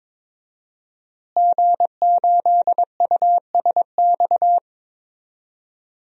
Morse code, heard as a keyed tone, about 22 wpm, 700 Hz. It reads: G8USX
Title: DXCC-100 Call Signs - 5 Characters - Repeated 3x - 22wpm